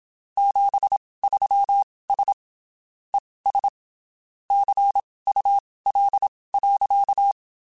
Morse code, heard as a keyed tone, about 27 wpm, spaced as for 19 wpm, 800 Hz. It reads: 73S ES CUL.
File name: (spaced) / QSOs Spaced-mp3s / kc7haa @25 sp